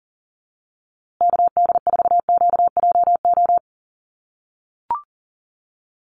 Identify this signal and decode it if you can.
Morse code, heard as a keyed tone, about 40 wpm, 700 Hz. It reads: KD4QJO
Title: USA Call Signs Repeated 2x 40wpm